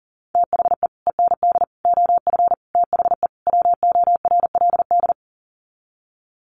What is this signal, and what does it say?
Morse code, heard as a keyed tone, about 40 wpm, 700 Hz. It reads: THE END OF THE WORLD
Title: Sets of 5 Words 40wpm